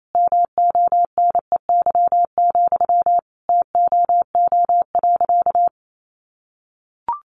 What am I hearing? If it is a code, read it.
Morse code, heard as a keyed tone, about 28 wpm, 700 Hz. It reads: MONEY, TOO.
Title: Sentences from Top 900 Words 28wpm